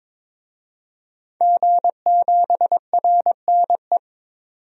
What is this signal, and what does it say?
Morse code, heard as a keyed tone, about 22 wpm, 700 Hz. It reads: G7RNE
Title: DXCC-100 Call Signs - 5 Characters - 22wpm